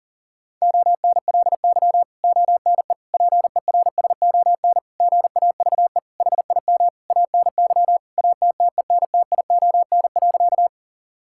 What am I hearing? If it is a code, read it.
Morse code, heard as a keyed tone, about 37 wpm, 700 Hz. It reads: ONLY ONE PERSON GAVE HIM ANY ATTENTION.
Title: Sentences from Top 1000 Words - Review Part 6 - 40wpm